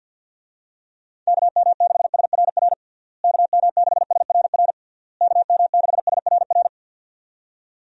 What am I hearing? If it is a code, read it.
Morse code, heard as a keyed tone, about 52 wpm, 700 Hz. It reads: KM6SRR KM6SRR KM6SRR
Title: USA Call Signs - 6 Characters - Repeated 3x - 50wpm